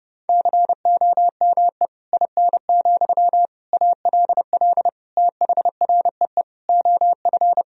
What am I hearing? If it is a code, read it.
Morse code, heard as a keyed tone, about 30 wpm, 700 Hz. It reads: COME IN, ALL THREE OF
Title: Sentences from Top 1000 Words - Review Part 1 - 30wpm